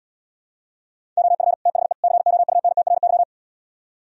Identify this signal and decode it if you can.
Morse code, heard as a keyed tone, about 77 wpm, 700 Hz. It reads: OF THE COUNTRY